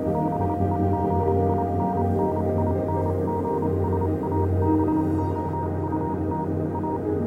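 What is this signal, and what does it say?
Morse code, heard as a keyed tone, about 28 wpm, spaced as for 25 wpm, 940 Hz, with noise. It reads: BEGINNINGS ABOUT A
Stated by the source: Morse code version of a music track.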